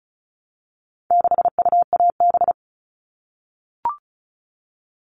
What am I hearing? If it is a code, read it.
Morse code, heard as a keyed tone, about 34 wpm, 700 Hz. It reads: 6UAB
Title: All Cut Numbers - 4-Digit 35wpm